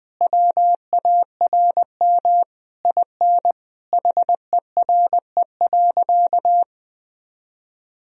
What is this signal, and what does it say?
Morse code, heard as a keyed tone, about 20 wpm, 700 Hz. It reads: WARM IN HERE.
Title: Sentences from Top 1000 Words - Review Part 4 - 20wpm